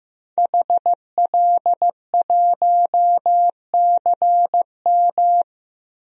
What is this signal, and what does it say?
Morse code, heard as a keyed tone, about 15 wpm, 700 Hz. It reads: HL1CM
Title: DXCC-100 Call Signs - 5 Characters - 15wpm